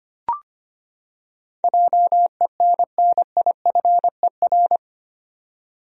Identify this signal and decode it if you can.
Morse code, heard as a keyed tone, about 25 wpm, 700 Hz. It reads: JENNIFER
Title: USA Common Names - 25wpm